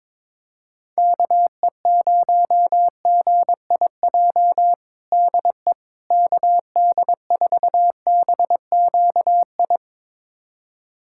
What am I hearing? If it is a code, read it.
Morse code, heard as a keyed tone, about 22 wpm, 700 Hz. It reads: KE0GIJ DE KD4BQI
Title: USA Callsign DE Callsign - 6 Characters - 22wpm